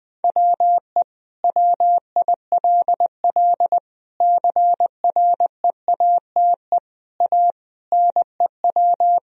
Morse code, heard as a keyed tone, about 20 wpm, 700 Hz. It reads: WE WILL CREATE A NEW
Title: Sentences from Top 1000 Words - Review Part 3 - 20wpm